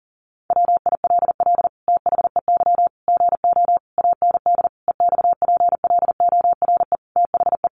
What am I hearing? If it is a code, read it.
Morse code, heard as a keyed tone, about 40 wpm, 700 Hz. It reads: WILL THEY GO AND EXPLORE THE
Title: Sentences from Top 1100 Words 40wpm